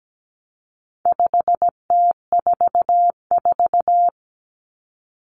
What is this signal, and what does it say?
Morse code, heard as a keyed tone, about 17 wpm, 700 Hz. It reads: 5T44
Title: Common Cut Numbers - 4-Digit 17wpm